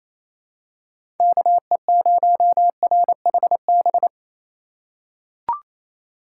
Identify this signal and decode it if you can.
Morse code, heard as a keyed tone, about 28 wpm, 700 Hz. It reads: KE0RHB
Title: USA Call Signs - 6 Characters - 2x Character Spacing - 28wpm